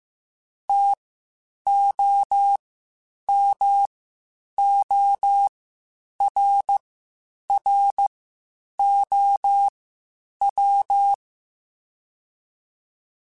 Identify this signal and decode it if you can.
Morse code, heard as a keyed tone, about 15 wpm, spaced as for 8 wpm, 800 Hz. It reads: TOMORROW
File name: Spaced / Chuckles Spaced-mp3s / SoftWords sp @13